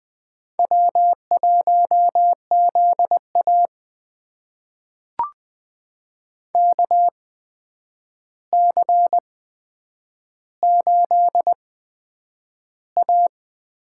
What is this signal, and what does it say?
Morse code, heard as a keyed tone, about 20 wpm, spaced as for 9 wpm, 700 Hz. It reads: W1ZA KC8A
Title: USA Call Signs - 8x Character Spacing - 20wpm